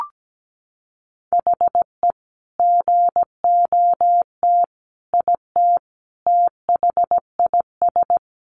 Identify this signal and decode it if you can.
Morse code, heard as a keyed tone, about 17 wpm, 700 Hz. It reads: HE GOT IT THIS
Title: Sentences from Top 400 Words 17wpm